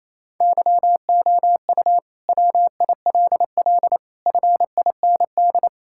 Morse code, heard as a keyed tone, about 28 wpm, 700 Hz. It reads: YOU WILL FIND
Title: Sentences from Top 1000 Words - Review Part 5 - 28wpm